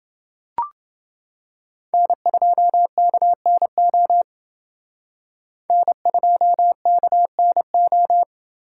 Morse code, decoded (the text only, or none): N2KNO N2KNO